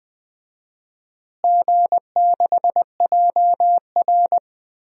Morse code, heard as a keyed tone, about 20 wpm, 700 Hz. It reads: G6JR
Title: DXCC-100 Call Signs - 4 Characters - Repeated 3x - 20wpm